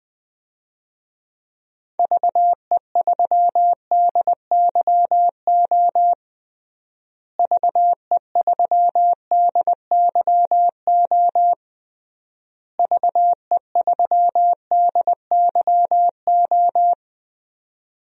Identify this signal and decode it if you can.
Morse code, heard as a keyed tone, about 20 wpm, 700 Hz. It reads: VE3DYO VE3DYO VE3DYO
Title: DXCC-100 Call Signs - 6 Characters - Repeated 3x - 20wpm